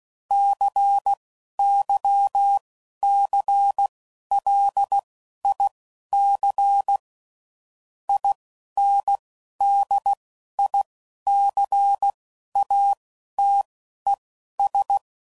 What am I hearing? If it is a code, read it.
Morse code, heard as a keyed tone, about 16 wpm, spaced as for 11 wpm, 800 Hz. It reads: CYCLIC INDICATES
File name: CW Code Groups @10WPM / Technical Subjects / XL